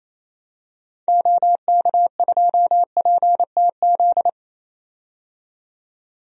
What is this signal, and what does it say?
Morse code, heard as a keyed tone, about 28 wpm, 700 Hz. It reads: OK2PTZ
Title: DXCC-100 Call Signs - 6 Characters - 28wpm